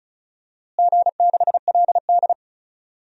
Morse code, heard as a keyed tone, about 36 wpm, 700 Hz. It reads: G6LD
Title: DXCC-100 Call Signs - 4 Characters - Repeated 3x - 35wpm